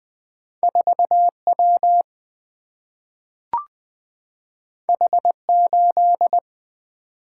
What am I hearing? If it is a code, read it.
Morse code, heard as a keyed tone, about 20 wpm, 700 Hz. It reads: 4W H8
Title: Single-Letter Single-Number 20wpm